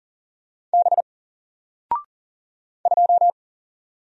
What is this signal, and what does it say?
Morse code, heard as a keyed tone, about 40 wpm, 700 Hz. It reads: B 2
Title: Single Letter-Number Alt 40wpm